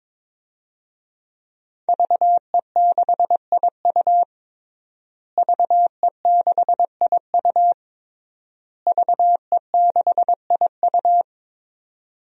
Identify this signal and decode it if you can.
Morse code, heard as a keyed tone, about 22 wpm, 700 Hz. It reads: VE6IU VE6IU VE6IU
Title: DXCC-100 Call Signs - 5 Characters - Repeated 3x - 22wpm